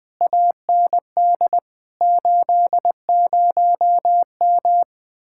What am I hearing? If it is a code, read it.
Morse code, heard as a keyed tone, about 20 wpm, 700 Hz. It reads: AND 80M
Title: Speed-Racing Common QSO Phrases Encore - 20wpm